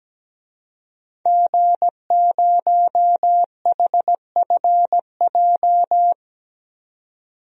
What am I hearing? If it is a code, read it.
Morse code, heard as a keyed tone, about 17 wpm, 700 Hz. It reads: G0HFJ
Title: DXCC-100 Call Signs - 5 Characters - 17wpm